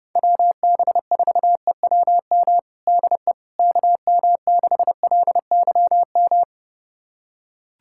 Morse code, heard as a keyed tone, about 30 wpm, 700 Hz. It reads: WB4EWM DE KM6LYM
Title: USA Callsign DE Callsign 30wpm